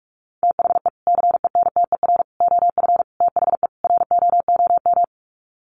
Morse code, heard as a keyed tone, about 44 wpm, 700 Hz. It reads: THE CENTER OF THE ROOM
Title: Sets of 5 Words Encore 45wpm